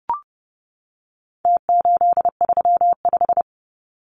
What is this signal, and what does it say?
Morse code, heard as a keyed tone, about 30 wpm, 700 Hz. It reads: T835
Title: Numbers - Contest Style - 4-Digit 30wpm